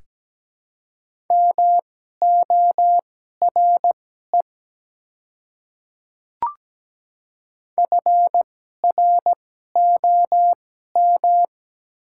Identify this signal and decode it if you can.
Morse code, heard as a keyed tone, about 17 wpm, spaced as for 12 wpm, 690 Hz. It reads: MORE FROM
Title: 2x Character Spacing - Top 100 Words - 17wpm